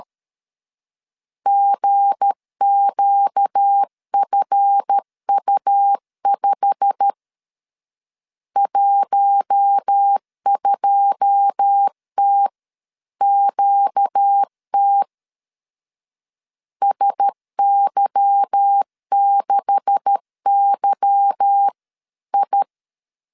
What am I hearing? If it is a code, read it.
Morse code, heard as a keyed tone, about 13 wpm, 800 Hz. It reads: GQFU5 12TQT SY6YI
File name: CW / DET / 1 9 1500C S A 12wpm 3800